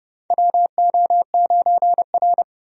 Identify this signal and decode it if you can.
Morse code, heard as a keyed tone, about 29 wpm, 700 Hz. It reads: WO9R
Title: USA Call Signs - 4 Characters - 30wpm